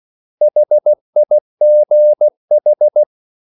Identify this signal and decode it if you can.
Morse code, heard as a keyed tone, about 16 wpm, 600 Hz. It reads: HIGH
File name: CW / Practice / 500MostCommonEnglishWords16